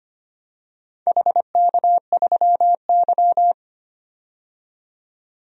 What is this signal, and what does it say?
Morse code, heard as a keyed tone, about 26 wpm, 700 Hz. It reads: HK3Y
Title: DXCC-100 Call Signs - 4 Characters - 25wpm